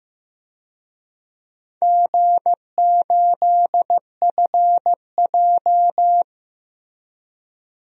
Morse code, heard as a keyed tone, about 15 wpm, 700 Hz. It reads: G8FJ
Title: DXCC-100 Call Signs - 4 Characters - Repeated 3x - 15wpm